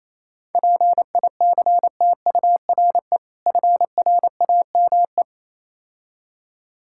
Morse code, heard as a keyed tone, about 28 wpm, 690 Hz. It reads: PICTURE FRAME